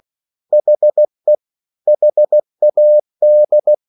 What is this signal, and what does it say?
Morse code, heard as a keyed tone, about 16 wpm, 600 Hz. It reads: HE HAD